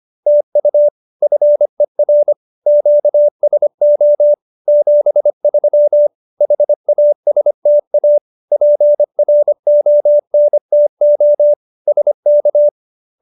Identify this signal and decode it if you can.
Morse code, heard as a keyed tone, about 25 wpm, 590 Hz. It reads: TU FER QSO 73 HASTA PRONTO SK